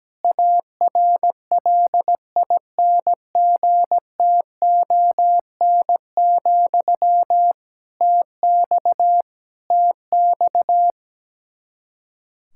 Morse code, heard as a keyed tone, about 17 wpm, 700 Hz. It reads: ARLINGTON, TX TX
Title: K1USN SST QSO Practice 17wpm